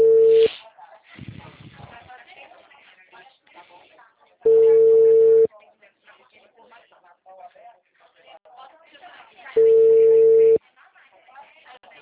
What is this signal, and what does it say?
Ringback tone (China, 450 Hz), with speech.
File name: out- -2006- - - .33871